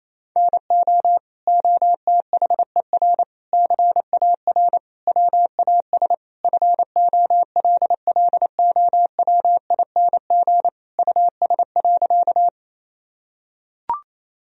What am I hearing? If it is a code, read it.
Morse code, heard as a keyed tone, about 28 wpm, 700 Hz. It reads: NO OTHER CAR WAS FOLLOWING US.
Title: Sentences from Top 1000 Words - Review Part 4 - 28wpm